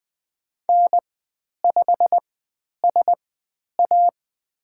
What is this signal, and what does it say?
Morse code, heard as a keyed tone, about 20 wpm, spaced as for 10 wpm, 700 Hz. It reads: N5SA
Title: 102 Lesson 17g - Letter u - Call Signs 20wpm